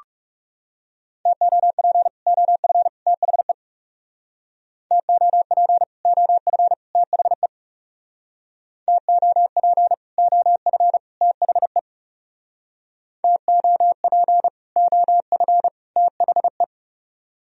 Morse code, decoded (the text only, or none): TOP OF THE TOP OF THE TOP OF THE TOP OF THE